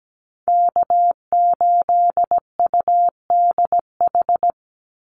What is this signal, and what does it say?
Morse code, heard as a keyed tone, about 17 wpm, 700 Hz. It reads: K8UDH
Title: Call Signs - 17wpm